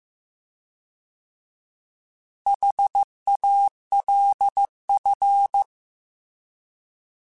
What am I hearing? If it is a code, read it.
Morse code, heard as a keyed tone, about 15 wpm, 800 Hz. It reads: HALF
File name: Alphabetical / Alphabetical mp3s / Hs @13